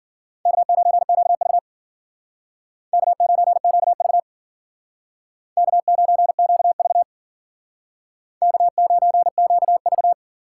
Morse code, heard as a keyed tone, about 51 wpm, 700 Hz. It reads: K9QV K9QV K9QV K9QV